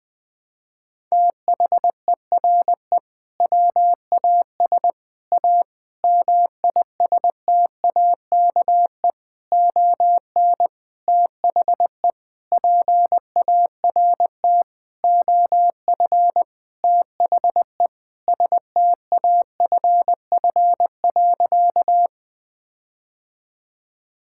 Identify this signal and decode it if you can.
Morse code, heard as a keyed tone, about 21 wpm, 700 Hz. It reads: THERE WAS A MISTAKE ON THE PART OF THE STAFF.
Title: Sentences with 5-Word Phrases - Spoken Beginning - 20wpm